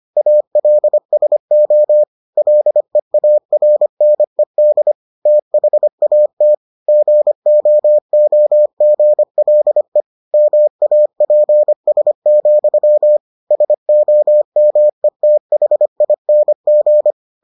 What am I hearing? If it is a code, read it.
Morse code, heard as a keyed tone, about 25 wpm, 600 Hz. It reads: ALSO LEARNED THAT GOOGLE MAPS, SOMETHING